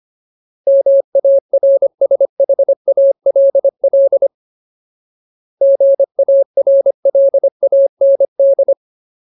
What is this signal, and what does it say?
Morse code, heard as a keyed tone, about 25 wpm, 550 Hz. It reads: MARSHALL GARLAND
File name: Ultimate Word List / 25WPM Ultimate